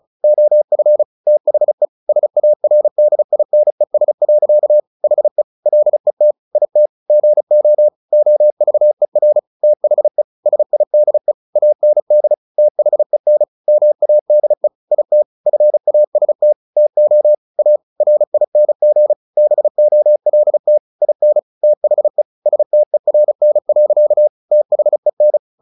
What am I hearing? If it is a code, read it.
Morse code, heard as a keyed tone, about 34 wpm, 600 Hz. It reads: OF THE SARDINES. HE LET IT GO OVER THE SIDE AND THEN MADE IT FAST TO A RING BOLT IN THE STERN. THEN
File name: the old man and the sea / the old man and the sea 1 35wpm